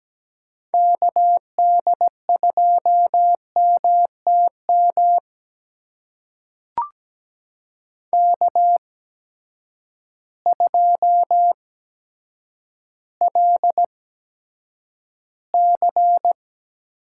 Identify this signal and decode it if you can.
Morse code, heard as a keyed tone, about 17 wpm, spaced as for 6 wpm, 700 Hz. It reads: KD2MT MK2LC
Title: USA Call Signs - 8x Character Spacing - 17wpm